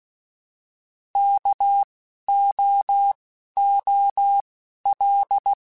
Morse code, heard as a keyed tone, about 16 wpm, spaced as for 11 wpm, 800 Hz. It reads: KOOL